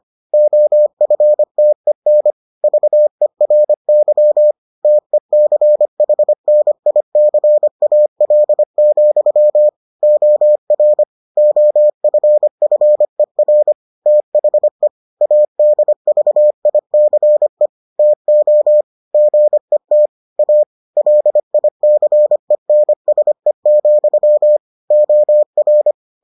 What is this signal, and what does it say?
Morse code, heard as a keyed tone, about 25 wpm, 600 Hz. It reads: OFTEN VERY TECHNICAL, OR OFFER THE ADVICE TO GET A LICENSE, OR